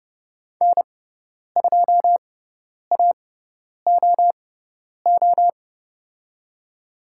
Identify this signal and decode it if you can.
Morse code, heard as a keyed tone, about 30 wpm, spaced as for 10 wpm, 700 Hz. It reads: N2AOO
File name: Lesson 16g - Letter c - Call Signs 30wpm